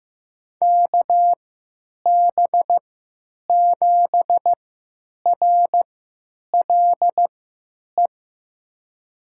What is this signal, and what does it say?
Morse code, heard as a keyed tone, about 15 wpm, spaced as for 8 wpm, 700 Hz. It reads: KB7RLE
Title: USA Call Signs - 6 Characters - 3x Character Spacing - 15wpm